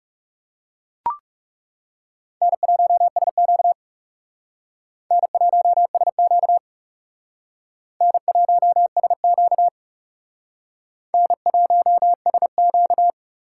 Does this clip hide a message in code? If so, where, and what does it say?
Yes: Morse code from 2.4 to 13.1 s, heard as a keyed tone, about 38 wpm, 700 Hz. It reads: N1SQ N1SQ N1SQ N1SQ